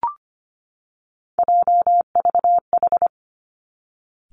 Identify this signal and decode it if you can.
Morse code, heard as a keyed tone, about 25 wpm, 700 Hz. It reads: JVH
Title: Three Challenging Letter-Number Combinations 25wpm